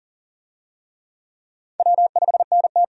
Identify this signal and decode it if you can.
Morse code, heard as a keyed tone, about 40 wpm, 700 Hz. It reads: W5NT
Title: USA Call Signs - 4 Characters - 40wpm